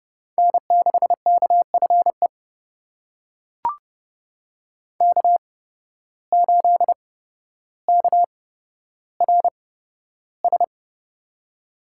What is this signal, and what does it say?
Morse code, heard as a keyed tone, about 30 wpm, spaced as for 13 wpm, 700 Hz. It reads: N6KFE K8KRS
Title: USA Call Signs - 5 Characters - 8x Character Spacing - 30wpm